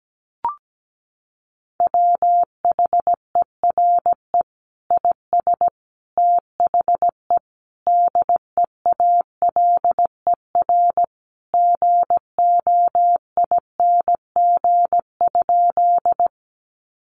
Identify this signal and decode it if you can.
Morse code, heard as a keyed tone, about 17 wpm, 700 Hz. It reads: WHERE IS THE DEALER GOING?
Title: Sentences from Top 1100 Words 17wpm